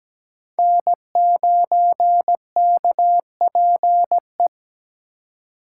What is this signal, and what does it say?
Morse code, heard as a keyed tone, about 20 wpm, 700 Hz. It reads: N9KPE
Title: Speed Racing - USA 1x3 Call Signs 17wpm